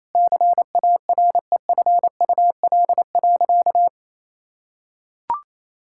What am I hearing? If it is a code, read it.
Morse code, heard as a keyed tone, about 28 wpm, 700 Hz. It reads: CAREFUL.
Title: Sentences from Top 1600 Words 28wpm